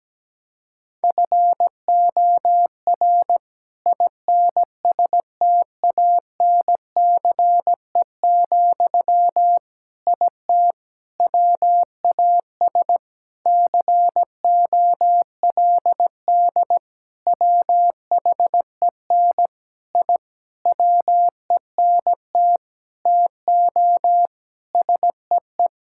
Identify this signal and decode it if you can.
Morse code, heard as a keyed tone, about 17 wpm, 700 Hz. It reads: FOR INSTANCE, IT WAS COLD WHEN I WENT TO SEE
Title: Sentences from Top 1000 Words - Review Part 5 - 17wpm